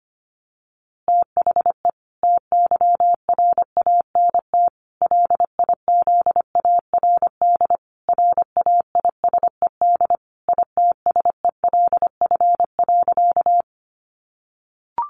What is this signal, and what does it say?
Morse code, heard as a keyed tone, about 25 wpm, 700 Hz. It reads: THE TYRANT LIZARD RAISED ITSELF.
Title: A Sound Of Thunder 25wpm